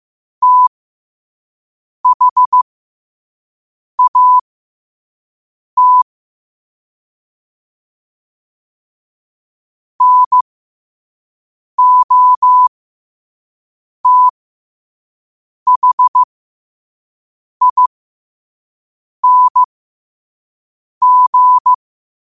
Morse code, heard as a keyed tone, about 15 wpm, spaced as for 5 wpm, 1000 Hz. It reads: THAT NOTHING
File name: qotdfiles / 5wpm